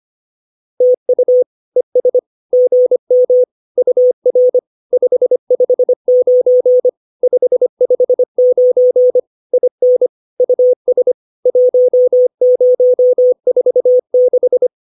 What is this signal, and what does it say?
Morse code, heard as a keyed tone, about 25 wpm, 500 Hz. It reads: TU ES GM UR 559 559 IN US 1046
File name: POTA310a 25